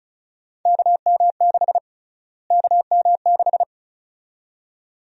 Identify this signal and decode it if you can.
Morse code, heard as a keyed tone, about 34 wpm, 700 Hz. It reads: KM6 KM6
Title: USA Call Sign Prefixes - Repeated 2x - 35wpm